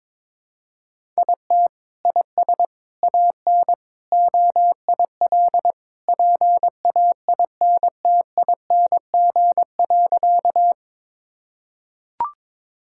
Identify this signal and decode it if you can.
Morse code, heard as a keyed tone, about 22 wpm, 700 Hz. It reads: IT IS AN OIL PAINTING.
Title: Sentences from Top 1000 Words - Review Part 6 - 22wpm